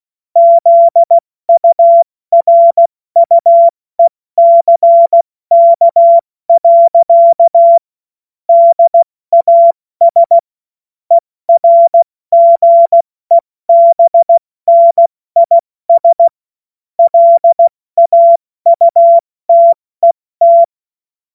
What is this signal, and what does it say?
Morse code, heard as a keyed tone, about 16 wpm, 680 Hz. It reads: ZURUECK. DAS ERGEBNIS LAUTET
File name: Morsekurs / Witze / Witz 21. Dezember 2021, 70 F80